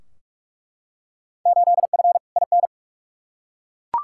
Morse code, heard as a keyed tone, about 42 wpm, 700 Hz. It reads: 8F IN
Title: Field Day Exchanges Encore - 45wpm